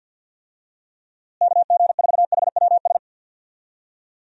Morse code, heard as a keyed tone, about 51 wpm, 700 Hz. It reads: KG4HWS
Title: USA Call Signs - 6 Characters - Repeated 2x - 50wpm